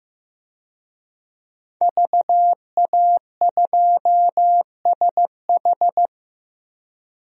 Morse code, heard as a keyed tone, about 15 wpm, 700 Hz. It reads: VA2SH
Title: DXCC-100 Call Signs - 5 Characters - Repeated 3x - 15wpm